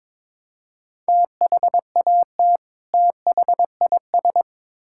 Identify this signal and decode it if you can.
Morse code, heard as a keyed tone, about 22 wpm, 700 Hz. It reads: THAT THIS
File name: Build A Phrase - 2-Word Phrases - 22wpm